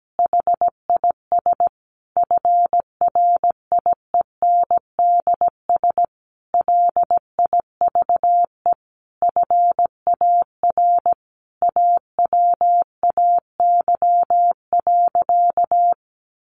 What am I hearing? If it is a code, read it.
Morse code, heard as a keyed tone, about 17 wpm, 700 Hz. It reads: HIS FRIENDS LIVE FAR AWAY.
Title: Additional Easy Phrases 17wpm